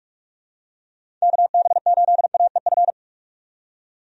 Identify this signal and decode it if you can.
Morse code, heard as a keyed tone, about 46 wpm, 700 Hz. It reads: KB8AEF
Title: USA Call Signs - 6 Characters - 45wpm